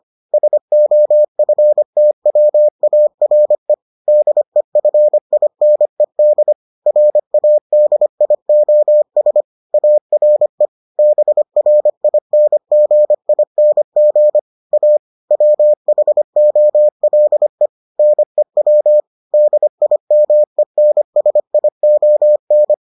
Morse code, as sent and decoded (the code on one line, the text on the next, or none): ... --- ..-. - .-- .- .-. / -.. ..-. .. -. -.. / .-. .- -.. .. --- ... / .- .-. / -... .-. .. -. --. .. -. --. / .- / .-- .... --- .-.. / -. .-- / -.. .. -- . -. ... .. --- -.
SOFTWARE DEFINED RADIOS ARE BRINGING A WHOLE NEW DIMENSION